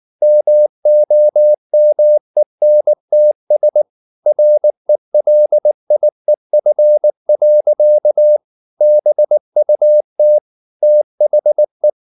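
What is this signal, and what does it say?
Morse code, heard as a keyed tone, about 19 wpm, 600 Hz. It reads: MOMENTS RELIEF. BUT THE
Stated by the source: Morse code at 19 wpm from CW med Gnister, reading Ken Follett's novel Never in English.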